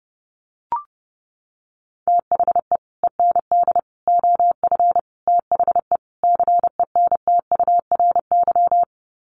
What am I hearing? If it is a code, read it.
Morse code, heard as a keyed tone, about 30 wpm, 700 Hz. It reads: THE END OF THE CENTURY
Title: Sets of 5 Words Encore2 30wpm